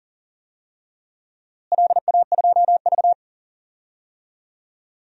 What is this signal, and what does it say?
Morse code, heard as a keyed tone, about 40 wpm, 700 Hz. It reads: LA2V
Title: DXCC-100 Call Signs - 4 Characters - Repeated 3x - 40wpm